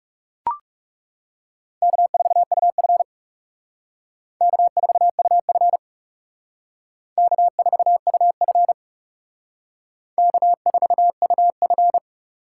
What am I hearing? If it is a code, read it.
Morse code, heard as a keyed tone, about 37 wpm, 700 Hz. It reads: K4UF K4UF K4UF K4UF
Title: Speed Racing - USA 1x2 Call Signs 30wpm